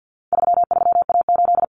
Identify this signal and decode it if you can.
Morse code, heard as a keyed tone, about 50 wpm, 700 Hz. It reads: 33A8
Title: Common Cut Numbers - 4-Digit 50wpm